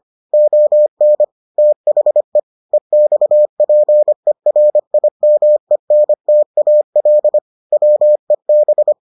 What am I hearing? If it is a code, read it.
Morse code, heard as a keyed tone, about 25 wpm, 600 Hz. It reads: ON THE EXPERIMENTAL WEB